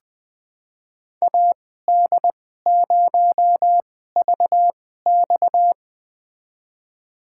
Morse code, heard as a keyed tone, about 20 wpm, spaced as for 15 wpm, 700 Hz. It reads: AD0VX
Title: USA Call Signs - 5 Characters - 2x Character Spacing - 20wpm